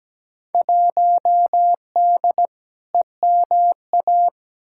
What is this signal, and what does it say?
Morse code, heard as a keyed tone, about 17 wpm, 700 Hz. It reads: 1D EMA
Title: Field Day Exchanges - 17wpm